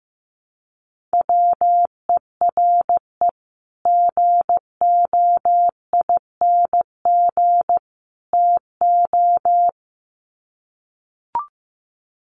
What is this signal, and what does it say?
Morse code, heard as a keyed tone, about 15 wpm, 700 Hz. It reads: WERE GOING TO
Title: Sets of 3 Words 15wpm